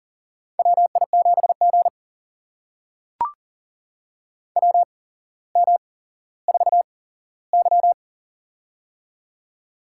Morse code, heard as a keyed tone, about 40 wpm, spaced as for 17 wpm, 700 Hz. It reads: WI7G WM4Y
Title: USA Call Signs - 4 Characters - 8x Character Spacing - 40wpm